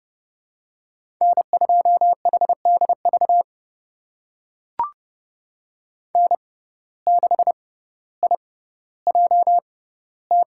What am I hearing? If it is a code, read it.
Morse code, heard as a keyed tone, about 30 wpm, spaced as for 15 wpm, 690 Hz. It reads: N2HDV N6IJT
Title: USA Call Signs - 5 Characters - 6x Character Spacing - 30wpm